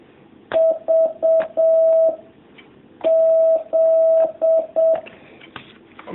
Morse code, heard as a keyed tone, about 7 wpm, 650 Hz. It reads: V Z
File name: jh8cbh / 2019cw